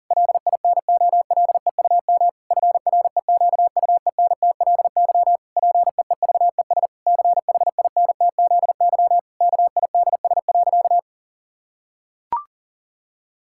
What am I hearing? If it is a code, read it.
Morse code, heard as a keyed tone, about 40 wpm, 700 Hz. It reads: LINOLEUM FREQUENTLY PEEVES CHINTZY KIDS.
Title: Pangram Challenge v2 - 40wpm